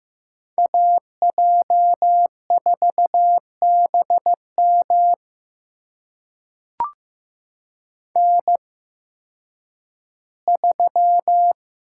Morse code, heard as a keyed tone, about 15 wpm, spaced as for 10 wpm, 700 Hz. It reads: AJ4BM N3
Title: USA Call Signs - 8x Character Spacing - 15wpm